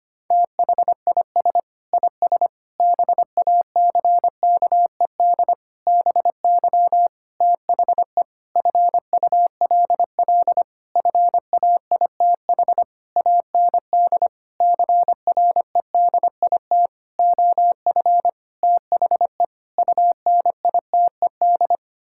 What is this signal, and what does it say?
Morse code, heard as a keyed tone, about 25 wpm, 700 Hz. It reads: THIS IS BACKED BY THE FULL FAITH AND CREDIT OF THE UNITED
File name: practice / 25wpm / Sentences with 9-Word Phrases - 25wpm